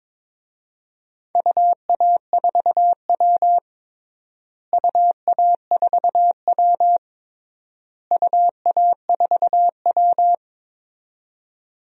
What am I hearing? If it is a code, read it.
Morse code, heard as a keyed tone, about 21 wpm, 700 Hz. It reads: UA4W UA4W UA4W
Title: DXCC-100 Call Signs - 4 Characters - Repeated 3x - 22wpm